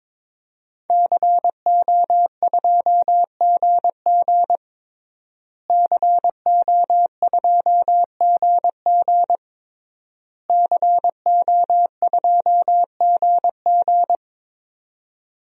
Morse code, heard as a keyed tone, about 22 wpm, 700 Hz. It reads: CO2GG CO2GG CO2GG
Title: DXCC-100 Call Signs - 5 Characters - Repeated 3x - 22wpm